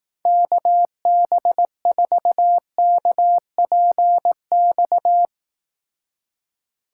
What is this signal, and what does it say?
Morse code, heard as a keyed tone, about 18 wpm, 700 Hz. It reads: KB4KPX
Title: Speed Racing - USA 2x3 Call Signs 15wpm